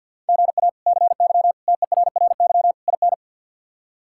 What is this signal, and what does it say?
Morse code, heard as a keyed tone, about 50 wpm, 700 Hz. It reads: GA CY TERRY IN
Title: K1USN SST QSO Practice - 50wpm